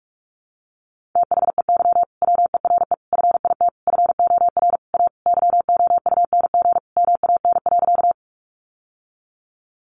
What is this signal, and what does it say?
Morse code, heard as a keyed tone, about 44 wpm, 700 Hz. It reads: THEY WERE FIT FOR A YOUNG MAN.
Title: Sentences from Top 800 Words 45wpm